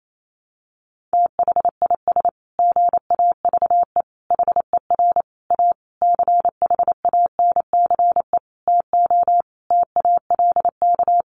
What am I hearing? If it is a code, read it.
Morse code, heard as a keyed tone, about 28 wpm, 700 Hz. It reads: THIS GAVE HER A CHANCE TO TALK
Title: Sentences from Top 1400 Words 28wpm